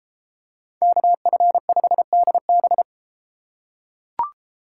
Morse code, heard as a keyed tone, about 33 wpm, 700 Hz. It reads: KF5DB